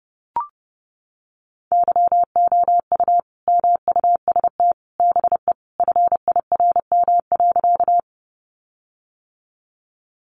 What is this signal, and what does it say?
Morse code, heard as a keyed tone, about 28 wpm, 700 Hz. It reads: YOU MUST BE FIRM.
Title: Sentences from Top 1400 Words 30wpm